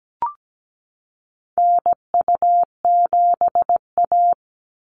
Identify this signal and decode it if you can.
Morse code, heard as a keyed tone, about 17 wpm, 700 Hz. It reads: NU7A
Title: NA SOTA Chaser Callsigns 17wpm